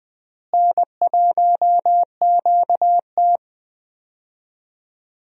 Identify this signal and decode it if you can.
Morse code, heard as a keyed tone, about 20 wpm, 700 Hz. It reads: N1QT
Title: USA Call Signs - 4 Characters - Repeated 2x - 20wpm